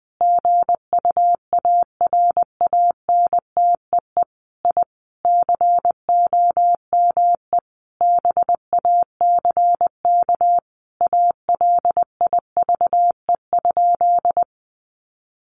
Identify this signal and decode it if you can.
Morse code, heard as a keyed tone, about 20 wpm, 700 Hz. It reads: GUARANTEE I COME BACK ALIVE?
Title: A Sound Of Thunder 20wpm